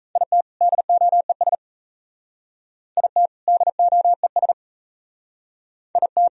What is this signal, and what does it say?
Morse code, heard as a keyed tone, about 36 wpm, 700 Hz. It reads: IT DOES IT DOES IT